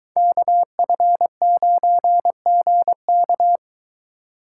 Morse code, heard as a keyed tone, about 23 wpm, 700 Hz. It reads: KF9GK